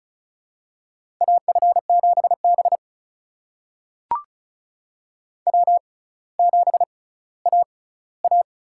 Morse code, heard as a keyed tone, about 36 wpm, spaced as for 18 wpm, 700 Hz. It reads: AF7B W7AA
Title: USA Call Signs - 4 Characters - 6x Character Spacing - 35wpm